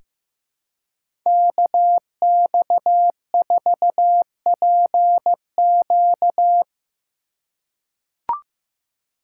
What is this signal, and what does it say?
Morse code, heard as a keyed tone, about 15 wpm, 700 Hz. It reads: KX4PQ
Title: K1USN SST QSO Practice - 15wpm